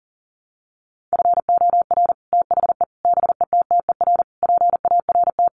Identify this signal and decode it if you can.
Morse code, heard as a keyed tone, about 40 wpm, 700 Hz. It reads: FOR THE BETTER PART
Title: Sets of 5 Words Encore2 40wpm